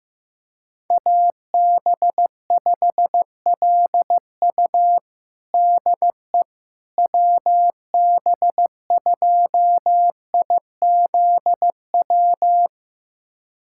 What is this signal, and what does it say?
Morse code, heard as a keyed tone, about 15 wpm, 700 Hz. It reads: AB5LU DE WB2IZW
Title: USA Callsign DE Callsign 15wpm